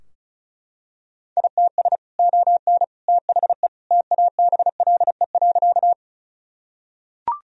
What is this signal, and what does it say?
Morse code, heard as a keyed tone, about 36 wpm, 700 Hz. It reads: ITS ON THE TABLE.
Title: Sentences from Top 1000 Words - Review Part 2 - 35wpm